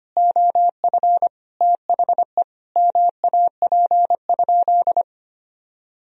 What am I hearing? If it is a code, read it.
Morse code, heard as a keyed tone, about 25 wpm, 700 Hz. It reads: OF THE MAP?
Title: Sentences from Top 1000 Words - Review Part 2 - 25wpm